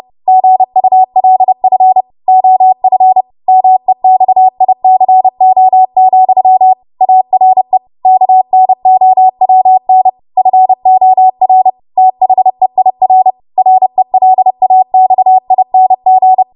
Morse code, heard as a keyed tone, about 30 wpm, 750 Hz. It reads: GULF OF MEXICO, ARE KNOWN FOR THEIR RELAXING
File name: Morse / 30WPM